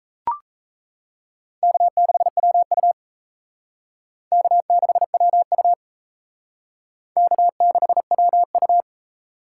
Morse code, heard as a keyed tone, about 34 wpm, 700 Hz. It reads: K6WU K6WU K6WU
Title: Speed Racing - USA 1x2 Call Signs 28wpm